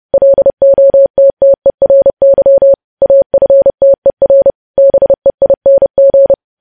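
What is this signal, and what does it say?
Morse code, heard as a keyed tone, about 30 wpm, 550 Hz. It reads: LOTTERY AFTER BEING
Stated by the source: News headlines sent as Morse practice at 30.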